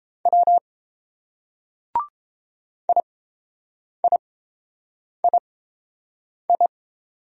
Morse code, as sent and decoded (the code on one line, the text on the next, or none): .-- / .. .. .. ..
W IIII